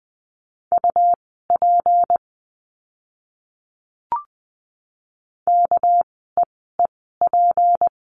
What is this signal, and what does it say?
Morse code, heard as a keyed tone, about 20 wpm, spaced as for 14 wpm, 700 Hz. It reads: UP KEEP
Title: 199 Lesson 32f - Letter k - Words 20wpm